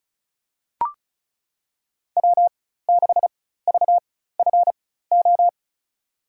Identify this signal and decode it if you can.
Morse code, heard as a keyed tone, about 36 wpm, spaced as for 16 wpm, 700 Hz. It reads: W6VFO
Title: USA Call Signs - 5 Characters - 4x Character Spacing - 35wpm